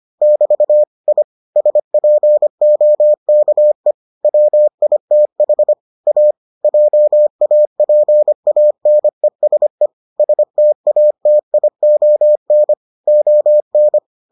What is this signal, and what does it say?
Morse code, heard as a keyed tone, about 25 wpm, 600 Hz. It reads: <BT> I SPOKE WITH A JAPANESE STATION ON